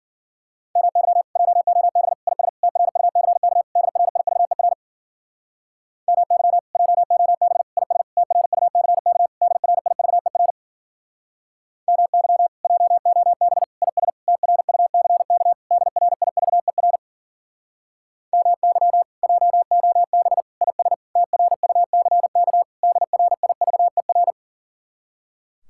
Morse code, heard as a keyed tone, about 52 wpm, 700 Hz. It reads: MY JOB IS TRUCK DRIVER MY JOB IS TRUCK DRIVER MY JOB IS TRUCK DRIVER MY JOB IS TRUCK DRIVER